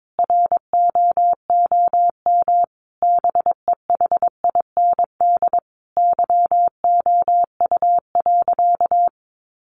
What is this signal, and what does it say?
Morse code, heard as a keyed tone, about 22 wpm, 700 Hz. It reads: ROOM BEHIND YOU.
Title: Sentences from Top 500 Words 22wpm